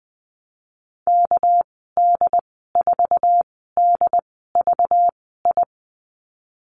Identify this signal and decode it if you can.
Morse code, heard as a keyed tone, about 20 wpm, spaced as for 15 wpm, 700 Hz. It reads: KD4DVI
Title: 213 Lesson 34g - Number 8 - Call Signs 20wpm